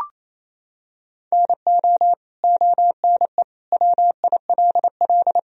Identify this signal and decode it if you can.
Morse code, heard as a keyed tone, about 28 wpm, 700 Hz. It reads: NO ONE WILL
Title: Sentences from Top 100 Words and Abbreviations 28wpm